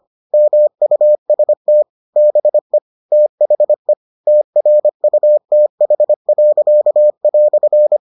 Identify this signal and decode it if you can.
Morse code, heard as a keyed tone, about 25 wpm, 600 Hz. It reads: MUST BE THE TRUTH."